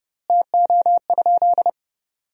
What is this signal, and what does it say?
Morse code, heard as a keyed tone, about 30 wpm, 700 Hz. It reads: TO?
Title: 3x Word-Spacing - Sentences from Top 100 Words - 30wpm